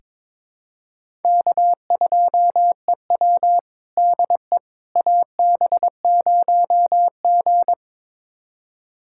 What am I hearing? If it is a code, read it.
Morse code, heard as a keyed tone, about 22 wpm, 700 Hz. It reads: K2EW DE AB0G
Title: USA Callsign DE Callsign - 4 Characters - 22wpm